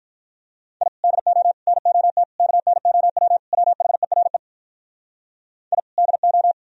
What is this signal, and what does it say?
Morse code, heard as a keyed tone, about 53 wpm, 700 Hz. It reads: I DO NOT KNOW WHERE I DO